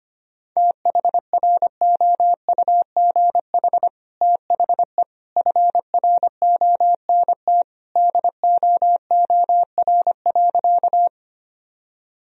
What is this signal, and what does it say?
Morse code, heard as a keyed tone, about 25 wpm, 700 Hz. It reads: THROUGH THE FRONT DOOR.
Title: Sentences from Top 1000 Words - Review Part 2 - 25wpm